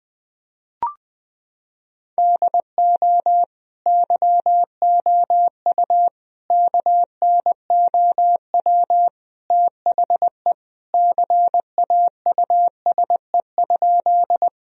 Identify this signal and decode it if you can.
Morse code, heard as a keyed tone, about 20 wpm, 700 Hz. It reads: DO YOU KNOW THE CAUSE?
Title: Sentences from Top 1000 Words - Review Part 1 - 20wpm